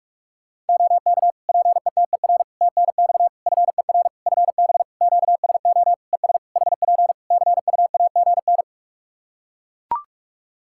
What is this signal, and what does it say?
Morse code, heard as a keyed tone, about 41 wpm, 700 Hz. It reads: OK PETER TNX FER FB QSO ES HP CUAGN
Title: Practice QSO - Exchange 4 - Ending - 45wpm